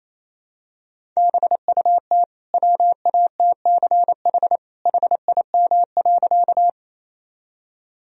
Morse code, heard as a keyed tone, about 28 wpm, 700 Hz. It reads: BUT WATCH HIM.
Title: Sentences from Top 1000 Words - Review Part 1 - 28wpm